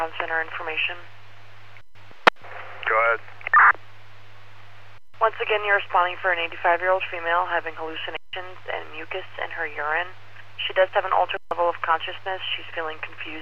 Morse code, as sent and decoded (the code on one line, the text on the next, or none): none